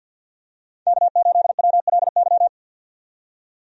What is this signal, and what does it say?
Morse code, heard as a keyed tone, about 47 wpm, 700 Hz. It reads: K8WLY
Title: USA Call Signs - 5 Characters - 50wpm